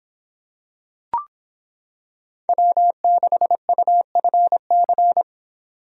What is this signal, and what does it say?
Morse code, heard as a keyed tone, about 26 wpm, 700 Hz. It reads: W6UFC